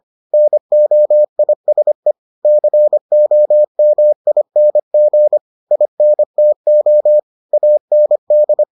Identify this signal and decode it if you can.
Morse code, heard as a keyed tone, about 25 wpm, 600 Hz. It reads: NOISE COMING INTO AND